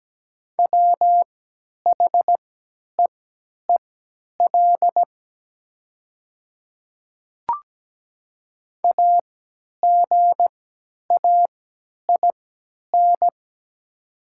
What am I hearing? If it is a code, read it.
Morse code, heard as a keyed tone, about 17 wpm, spaced as for 10 wpm, 690 Hz. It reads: WHEEL AGAIN
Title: 3x Character Spacing - Top 500 Words - 17wpm